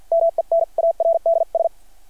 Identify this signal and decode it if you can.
Morse code, heard as a keyed tone, about 49 wpm, spaced as for 42 wpm, 660 Hz. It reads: MENARDS
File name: archived / CRSnet